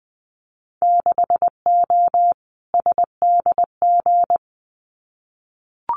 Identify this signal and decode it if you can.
Morse code, heard as a keyed tone, about 20 wpm, 700 Hz. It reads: 6O SDG
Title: Winter Field Day Exchanges Encore 20wpm